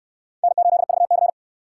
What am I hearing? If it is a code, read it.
Morse code, heard as a keyed tone, about 68 wpm, 700 Hz. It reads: N8VC